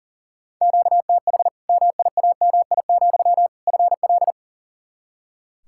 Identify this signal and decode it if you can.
Morse code, heard as a keyed tone, about 39 wpm, 700 Hz. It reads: QTH MIAMI, FL
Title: Practice QSO - Exchange 1 - Beginning - 40wpm